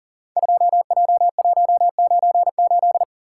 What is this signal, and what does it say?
Morse code, heard as a keyed tone, about 42 wpm, 700 Hz. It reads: 2J198
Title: Code Groups - 5 Letter:Numbers - 40wpm